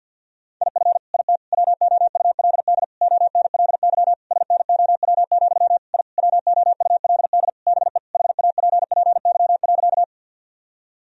Morse code, heard as a keyed tone, about 50 wpm, 700 Hz. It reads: IF IT WOULD ONLY SNOW, I WOULD BE HAPPY.
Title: Sentences from Top 1000 Words - Review Part 3 - 50wpm